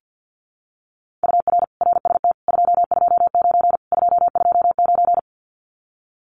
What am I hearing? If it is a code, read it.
Morse code, heard as a keyed tone, about 50 wpm, 700 Hz. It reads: UR RST 229 229
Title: Common QSO Phrases Encore 50wpm